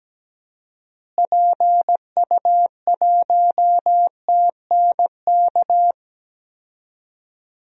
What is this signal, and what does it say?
Morse code, heard as a keyed tone, about 17 wpm, 700 Hz. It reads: PU1TNK
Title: DXCC-100 Call Signs - 6 Characters - Repeated 3x - 17wpm